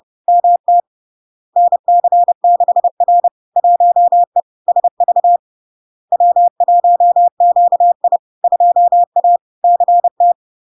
Morse code, heard as a keyed tone, about 30 wpm, 700 Hz. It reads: MT NC6R 1E SV W1QI 2A CT
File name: LARK-FD-2022-Log-in-CW-30wpm-700-Hz